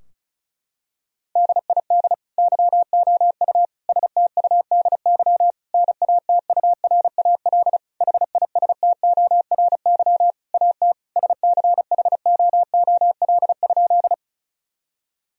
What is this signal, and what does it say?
Morse code, heard as a keyed tone, about 36 wpm, 700 Hz. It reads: DID YOU STUDY NATURAL HISTORY AT SCHOOL?
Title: Sentences from Top 1000 Words - Review Part 4 - 35wpm